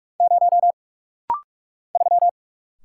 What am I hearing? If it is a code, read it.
Morse code, heard as a keyed tone, about 44 wpm, 700 Hz. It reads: 0 3
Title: Single Letter-Number - Rapid-Fire 45wpm